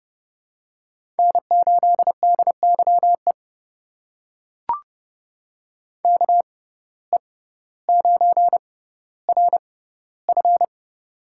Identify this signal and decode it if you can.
Morse code, heard as a keyed tone, about 30 wpm, spaced as for 15 wpm, 700 Hz. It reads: N8DYE KE9RF
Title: USA Call Signs - 5 Characters - 6x Character Spacing - 30wpm